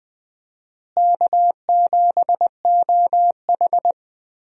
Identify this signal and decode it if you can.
Morse code, heard as a keyed tone, about 20 wpm, 690 Hz. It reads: K7OH
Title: USA Call Signs - 4 Characters - Repeated 3x - 20wpm